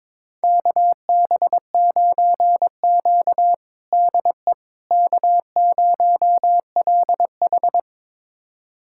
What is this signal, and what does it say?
Morse code, heard as a keyed tone, about 22 wpm, 700 Hz. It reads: KB9Q DE K0LH
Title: USA Callsign DE Callsign - 4 Characters - 22wpm